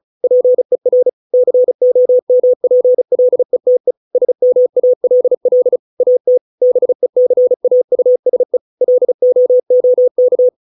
Morse code, heard as a keyed tone, about 34 wpm, 500 Hz. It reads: PER COMPLETE SMALL AT BECAUSE LOOK